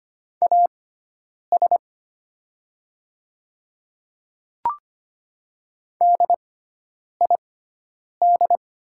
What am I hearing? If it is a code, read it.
Morse code, heard as a keyed tone, about 25 wpm, spaced as for 9 wpm, 700 Hz. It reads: AS DID
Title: 6x Character Spacing - Top 100 Words - 25wpm